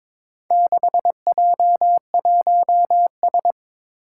Morse code, heard as a keyed tone, about 22 wpm, 700 Hz. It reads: 6J1S
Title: Four Challenging Letter-Number Combinations 22wpm